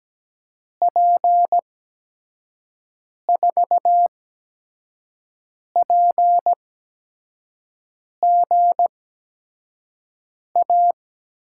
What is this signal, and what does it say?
Morse code, heard as a keyed tone, about 17 wpm, spaced as for 5 wpm, 700 Hz. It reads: P4PGA
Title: USA Call Signs - 8x Character Spacing - 17wpm